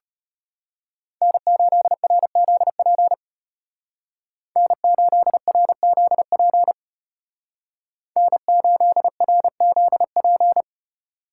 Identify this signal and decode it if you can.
Morse code, heard as a keyed tone, about 31 wpm, 700 Hz. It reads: N8RZP N8RZP N8RZP
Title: Speed Racing - USA 1x3 Call Signs 25wpm